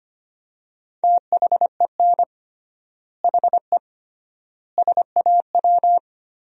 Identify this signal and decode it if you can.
Morse code, heard as a keyed tone, about 25 wpm, 700 Hz. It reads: THEN HE SAW
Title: 3x Word-Spacing - Sentences from Top 400 Words - 25wpm